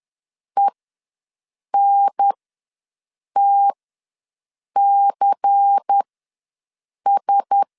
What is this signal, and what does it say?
Morse code, heard as a keyed tone, about 10 wpm, spaced as for 6 wpm, 800 Hz. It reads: ENTCS
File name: LEZIONI / 6 PKX0 / 6 P ws 10wpm 3751